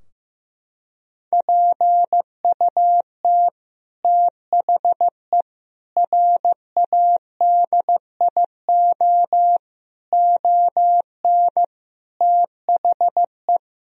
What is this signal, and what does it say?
Morse code, heard as a keyed tone, about 15 wpm, 700 Hz. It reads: PUT THE RADIO ON THE
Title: Sentences from Top 1000 Words - Review Part 3 - 15wpm